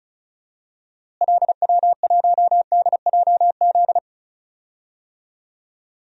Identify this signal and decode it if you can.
Morse code, heard as a keyed tone, about 36 wpm, 700 Hz. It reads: LW1DJZ
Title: DXCC-100 Call Signs - 6 Characters - Repeated 2x - 35wpm